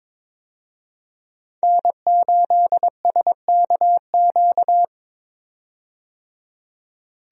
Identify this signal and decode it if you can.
Morse code, heard as a keyed tone, about 22 wpm, 700 Hz. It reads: N8SKQ
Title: USA Call Signs - 5 Characters - 22wpm